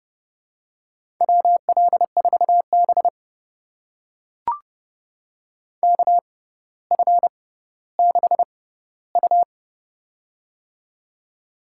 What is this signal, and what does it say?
Morse code, heard as a keyed tone, about 30 wpm, spaced as for 15 wpm, 700 Hz. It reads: WL4B KF6U
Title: USA Call Signs - 6x Character Spacing - 30wpm